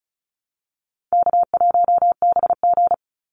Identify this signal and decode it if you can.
Morse code, heard as a keyed tone, about 34 wpm, 700 Hz. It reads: K1BG
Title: Call Signs - 35wpm